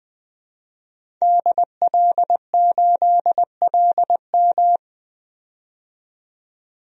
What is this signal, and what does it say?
Morse code, heard as a keyed tone, about 20 wpm, 700 Hz. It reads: DL8LM
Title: DXCC-100 Call Signs - 5 Characters - 20wpm